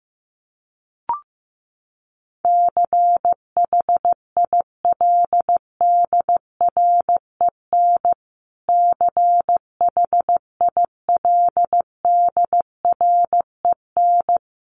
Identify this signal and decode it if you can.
Morse code, heard as a keyed tone, about 15 wpm, 700 Hz. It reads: CHILDREN CHILDREN
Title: Top 500 Words Repeated 15wpm